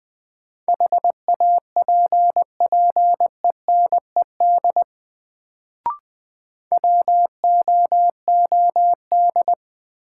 Morse code, heard as a keyed tone, about 20 wpm, 700 Hz. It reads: HAPPENED WOOD
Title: Sentences from Top 1000 Words - Review Part 2 - New Word Drill - 20wpm